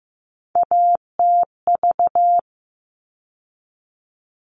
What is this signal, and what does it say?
Morse code, heard as a keyed tone, about 15 wpm, 700 Hz. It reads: ATV
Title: All Cut Numbers - 3-Digit 15wpm